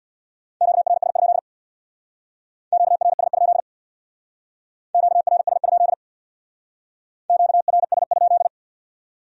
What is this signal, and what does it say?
Morse code, heard as a keyed tone, about 61 wpm, 700 Hz. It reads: QRS? QRS? QRS? QRS?
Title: Speed-Racing Q-Codes - 50wpm